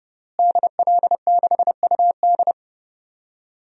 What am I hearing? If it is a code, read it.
Morse code, heard as a keyed tone, about 32 wpm, 700 Hz. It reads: DL6UD
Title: DXCC-100 Call Signs - 5 Characters - Repeated 3x - 30wpm